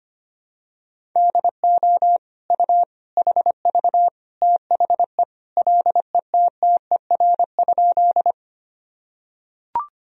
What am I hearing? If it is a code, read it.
Morse code, heard as a keyed tone, about 25 wpm, 700 Hz. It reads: DO U HV THE LETTER?
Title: Sentences from Top 300 Words and Abbreviations 25wpm